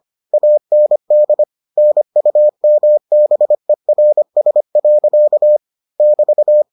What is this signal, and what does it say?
Morse code, heard as a keyed tone, about 25 wpm, 600 Hz. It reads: AND NUMBERS. <BT>